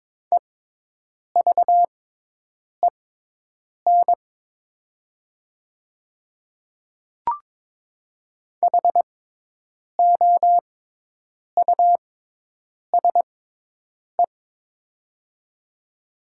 Morse code, heard as a keyed tone, about 22 wpm, spaced as for 8 wpm, 700 Hz. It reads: EVEN HOUSE
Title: 6x Character Spacing - Top 300 Words - 22wpm